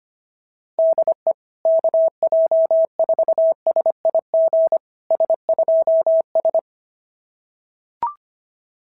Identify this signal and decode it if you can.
Morse code, heard as a keyed tone, about 24 wpm, 650 Hz. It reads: DE KJ4SIG S2S
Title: SOTA QSO Practice 25wpm